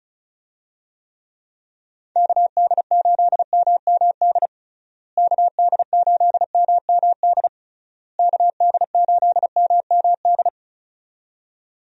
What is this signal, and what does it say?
Morse code, heard as a keyed tone, about 36 wpm, 700 Hz. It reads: KD8MMD KD8MMD KD8MMD
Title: USA Call Signs - 6 Characters - Repeated 3x - 35wpm